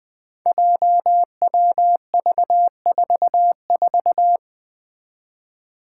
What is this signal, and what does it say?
Morse code, heard as a keyed tone, about 20 wpm, 700 Hz. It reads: JWV44
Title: Five Challenging Letter-Number Combinations 20wpm